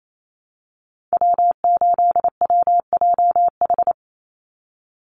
Morse code, heard as a keyed tone, about 28 wpm, 700 Hz. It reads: W8WJH
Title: USA Call Signs 28wpm